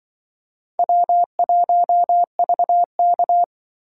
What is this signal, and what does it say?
Morse code, heard as a keyed tone, about 24 wpm, 700 Hz. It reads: W1VK
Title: Speed Racing - USA 1x2 Call Signs 20wpm